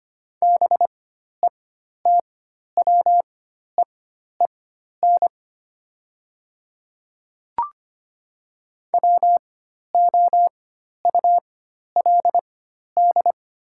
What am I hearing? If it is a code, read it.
Morse code, heard as a keyed tone, about 25 wpm, spaced as for 12 wpm, 700 Hz. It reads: BETWEEN WOULD
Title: 4x Character Spacing - Top 300 Words - 25wpm